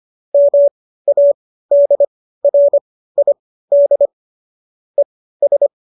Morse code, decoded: MADRID ES

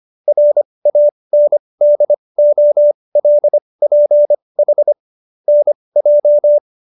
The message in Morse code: RANDOLPH NJ

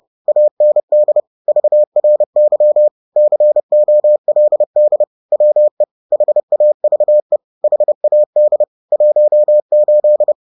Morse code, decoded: AND VRY COLD WE HAVE HAD 18

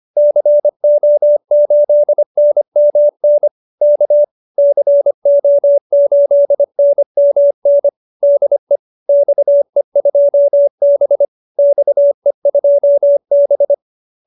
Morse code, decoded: CO8NMN K CO8NMN DE XE2B XE2B

